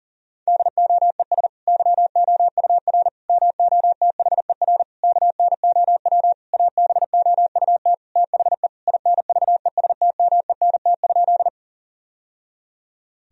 Morse code, heard as a keyed tone, about 37 wpm, 700 Hz. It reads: DOES YOUR MOTHER KNOW ABOUT THE INVESTMENT?
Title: Sentences from Top 1000 Words - Review Part 4 - 40wpm